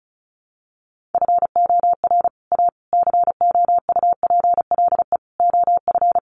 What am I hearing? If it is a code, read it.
Morse code, heard as a keyed tone, about 34 wpm, 700 Hz. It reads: FOR A COUPLE OF
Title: Sets of 5 Words Encore2 35wpm